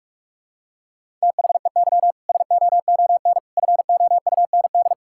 Morse code, heard as a keyed tone, about 41 wpm, 700 Hz. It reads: THEY SOON FOUND
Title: Sentences from Top 1000 Words - Review Part 5 - 45wpm